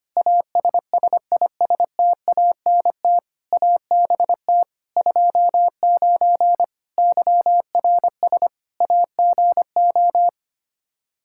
Morse code, heard as a keyed tone, about 25 wpm, 700 Hz. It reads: ASSISTANT ABT 29 YRS AGO
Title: Practice QSO - Exchange 3 - More Info - 25wpm